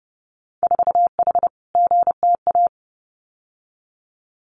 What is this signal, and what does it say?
Morse code, heard as a keyed tone, about 30 wpm, 700 Hz. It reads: 4H GTA
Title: Winter Field Day Exchanges Encore 30wpm